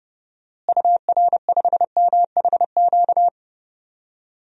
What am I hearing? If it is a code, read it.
Morse code, heard as a keyed tone, about 30 wpm, 700 Hz. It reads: UR5MHQ
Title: DXCC-100 Call Signs - 6 Characters - Repeated 3x - 30wpm